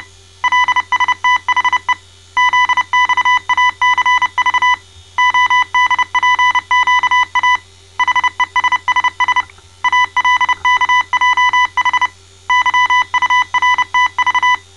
Morse code, heard as a keyed tone, about 29 wpm, 1000 Hz. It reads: LSTHE ZXACV ODPQA HESSS ALKJH YURTV
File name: mrx / malta